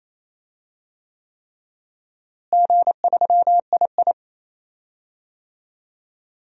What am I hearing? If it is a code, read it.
Morse code, heard as a keyed tone, about 27 wpm, 700 Hz. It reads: G3II
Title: DXCC-100 Call Signs - 4 Characters - Repeated 3x - 28wpm